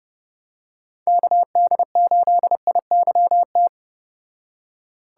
Morse code, heard as a keyed tone, about 30 wpm, 700 Hz. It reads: KD8IYT